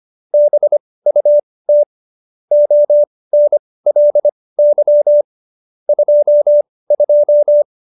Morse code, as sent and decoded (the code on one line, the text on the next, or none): -... ..- - / --- -. .-.. -.-- / ..--- ..---
BUT ONLY 22